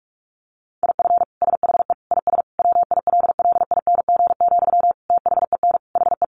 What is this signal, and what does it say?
Morse code, heard as a keyed tone, about 44 wpm, 700 Hz. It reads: IF SHE IS WILLING, THEN HE
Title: Sentences from Top 1000 Words 45wpm